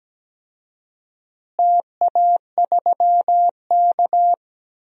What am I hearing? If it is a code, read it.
Morse code, heard as a keyed tone, about 17 wpm, 700 Hz. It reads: TA3K
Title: DXCC-100 Call Signs - 4 Characters - Repeated 3x - 17wpm